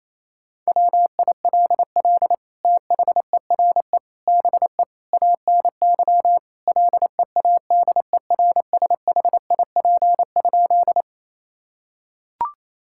Morse code, heard as a keyed tone, about 28 wpm, 700 Hz. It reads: WILL THERE BE ANY LEADERSHIP?
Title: Sentences from Top 1600 Words 28wpm